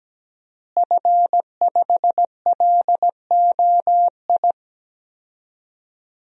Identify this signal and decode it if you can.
Morse code, heard as a keyed tone, about 17 wpm, 700 Hz. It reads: F5LOI
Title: DXCC-100 Call Signs - 5 Characters - Repeated 3x - 17wpm